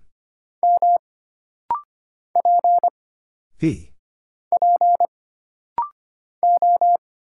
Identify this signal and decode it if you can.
Morse code, heard as a keyed tone, about 25 wpm, 700 Hz. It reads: M P P O
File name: Single Character (Letter:Number:Punctuation:Prosign) - Rapid-Fire - 25wpm